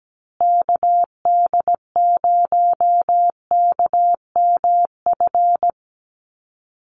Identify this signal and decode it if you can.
Morse code, heard as a keyed tone, about 17 wpm, 700 Hz. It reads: KD0KMF
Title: USA Call Signs Repeated 2x 17wpm